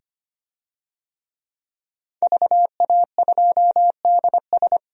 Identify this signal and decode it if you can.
Morse code, heard as a keyed tone, about 25 wpm, 690 Hz. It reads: VA2DS
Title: DXCC-100 Call Signs - 5 Characters - 25wpm